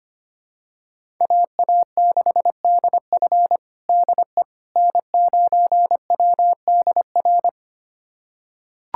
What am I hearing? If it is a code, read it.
Morse code, heard as a keyed tone, about 25 wpm, 700 Hz. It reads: AA6DF DE N9WDR
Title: USA Callsign DE Callsign - 5 Characters - 25wpm